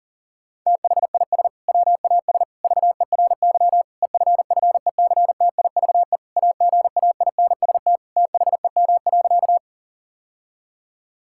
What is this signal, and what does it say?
Morse code, heard as a keyed tone, about 40 wpm, 700 Hz. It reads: THIS WAS VERY EFFECTIVE AGAINST THEM.
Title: Sentences from Top 1000 Words - Review Part 6 - 40wpm